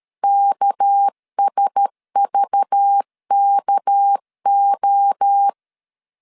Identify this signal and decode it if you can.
Morse code, heard as a keyed tone, about 13 wpm, 800 Hz. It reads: KSVKO
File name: LEZIONI / 6 PKX0 / 6 PK 12wpm 2006